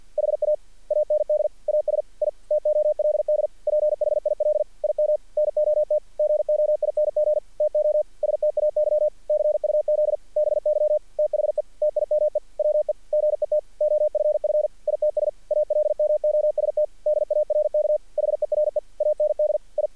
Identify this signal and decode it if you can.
Morse code, heard as a keyed tone, about 48 wpm, 610 Hz. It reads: HA AND AS I TOLD PHIL IM NOT GOING TO STAY CUZ BY THE TIME WE GET OFF ITS ALMOST DARK HERE AND I
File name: archived / CRSnet